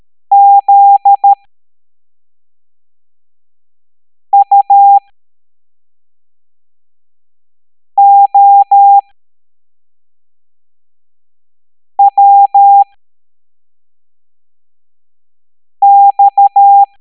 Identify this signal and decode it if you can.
Morse code, heard as a keyed tone, about 13 wpm, spaced as for 3 wpm, 800 Hz. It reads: ZUOWX